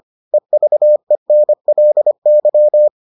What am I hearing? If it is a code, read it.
Morse code, heard as a keyed tone, about 25 wpm, 600 Hz. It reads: EVENLY